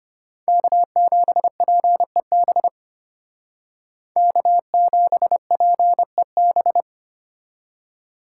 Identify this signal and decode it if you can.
Morse code, heard as a keyed tone, about 33 wpm, 700 Hz. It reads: K7PEB K7PEB